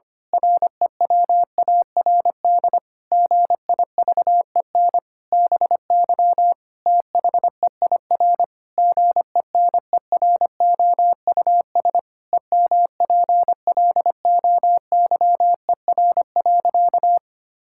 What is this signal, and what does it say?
Morse code, heard as a keyed tone, about 25 wpm, 700 Hz. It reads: REWARD GIVEN BY THEIR GENEROUS EMPLOYER.
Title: Pangram Challenge v2 - 25wpm